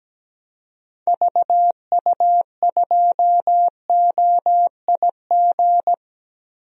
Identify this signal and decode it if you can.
Morse code, heard as a keyed tone, about 17 wpm, 700 Hz. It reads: VU2OIG
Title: DXCC-100 Call Signs - 6 Characters - Repeated 2x - 17wpm